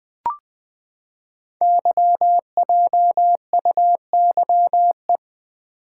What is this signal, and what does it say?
Morse code, heard as a keyed tone, about 20 wpm, 700 Hz. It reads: YJUYE
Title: Code Groups - 5 Letters - 20wpm